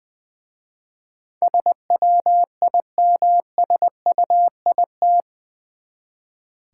Morse code, heard as a keyed tone, about 20 wpm, 700 Hz. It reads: SWIMSUIT
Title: Easy Words with Hints - 20wpm